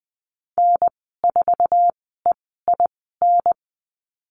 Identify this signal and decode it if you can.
Morse code, heard as a keyed tone, about 20 wpm, spaced as for 15 wpm, 700 Hz. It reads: N4EIN
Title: 048 Lesson 9g - Number 4 - Call Signs 20wpm